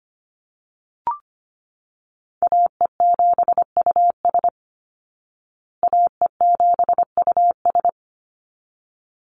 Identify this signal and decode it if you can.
Morse code, heard as a keyed tone, about 25 wpm, 700 Hz. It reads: AE7US AE7US
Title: Call Signs Repeated 3x - 25wpm